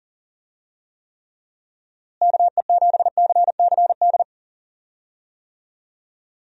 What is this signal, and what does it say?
Morse code, heard as a keyed tone, about 39 wpm, 700 Hz. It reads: KE7CCD
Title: USA Call Signs - 6 Characters - Repeated 2x - 40wpm